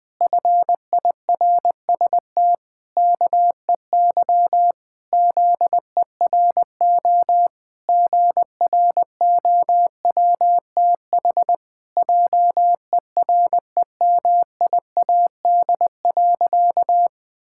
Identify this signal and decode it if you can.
Morse code, heard as a keyed tone, about 21 wpm, 700 Hz. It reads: FIRST KEY ZERO GROWTH JEREMIAD.
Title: Pangram Challenge v2 - 20wpm